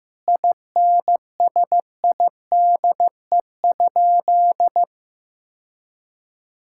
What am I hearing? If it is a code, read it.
Morse code, heard as a keyed tone, about 16 wpm, 700 Hz. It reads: INSIDE?
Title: Sentences from Top 1000 Words - Review Part 6 - 15wpm